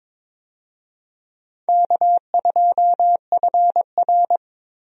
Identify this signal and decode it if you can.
Morse code, heard as a keyed tone, about 22 wpm, 700 Hz. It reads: K2FR
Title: USA Call Signs - 4 Characters - Repeated 2x - 22wpm